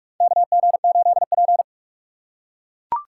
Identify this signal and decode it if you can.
Morse code, heard as a keyed tone, about 41 wpm, 690 Hz. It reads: KG8P
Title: USA Call Signs - 4 Characters - 2x Character Spacing - 45wpm